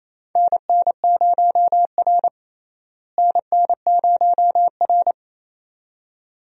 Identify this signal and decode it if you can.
Morse code, heard as a keyed tone, about 28 wpm, 700 Hz. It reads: NN0R NN0R
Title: USA Call Signs - 4 Characters - Repeated 2x - 28wpm